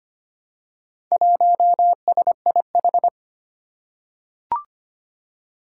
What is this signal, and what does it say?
Morse code, heard as a keyed tone, about 25 wpm, 700 Hz. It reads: 1SIH
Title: Four Challenging Letter-Number Combinations 25wpm